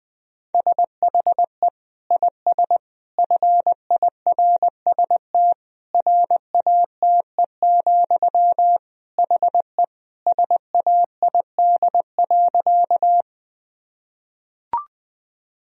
Morse code, heard as a keyed tone, about 20 wpm, 700 Hz. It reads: SHE IS FIRST RATE, HE SAID.
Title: Sentences from Top 1000 Words - Review Part 4 - 20wpm